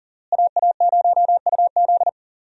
Morse code, heard as a keyed tone, about 40 wpm, 700 Hz. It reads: AA0UZ
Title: USA Call Signs - 5 Characters - Repeated 2x - 40wpm